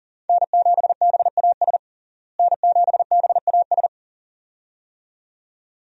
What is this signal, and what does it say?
Morse code, heard as a keyed tone, about 37 wpm, 700 Hz. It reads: N7BAS N7BAS
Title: USA Call Signs - 5 Characters - Repeated 2x - 40wpm